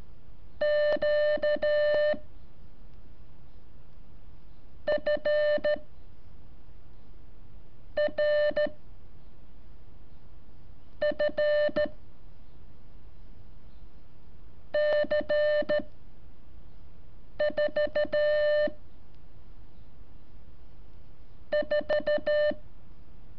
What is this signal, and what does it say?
Morse code, heard as a keyed tone, about 13 wpm, 600 Hz. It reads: Q F R F C 4 4
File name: curso / ejercicio 21